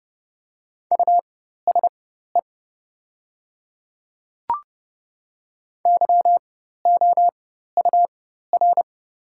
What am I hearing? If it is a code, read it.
Morse code, heard as a keyed tone, about 30 wpm, spaced as for 14 wpm, 700 Hz. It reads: USE YOUR
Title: 4x Character Spacing - Top 100 Words - 30wpm